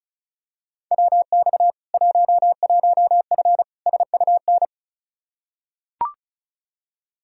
Morse code, heard as a keyed tone, about 36 wpm, 700 Hz. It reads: WX 11F SUN
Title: Practice QSO - Exchange 2 - Info - 35wpm